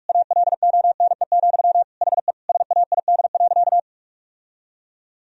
Morse code, heard as a keyed tone, about 41 wpm, 700 Hz. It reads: ALONE, HE SAID.
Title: Sentences from Top 1000 Words - Review Part 3 - 45wpm